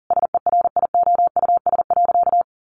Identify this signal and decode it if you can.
Morse code, heard as a keyed tone, about 40 wpm, 700 Hz. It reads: SERIOUS.
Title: Sentences from Top 1100 Words 40wpm